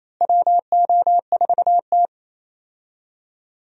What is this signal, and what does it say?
Morse code, heard as a keyed tone, about 28 wpm, 700 Hz. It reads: WO4T
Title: USA Call Signs - 4 Characters - Repeated 3x - 28wpm